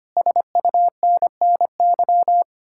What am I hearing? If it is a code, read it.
Morse code, heard as a keyed tone, about 33 wpm, 700 Hz. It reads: SUNNY